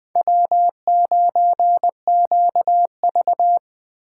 Morse code, heard as a keyed tone, about 20 wpm, 700 Hz. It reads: W9QV
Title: Speed Racing - USA 1x2 Call Signs 17wpm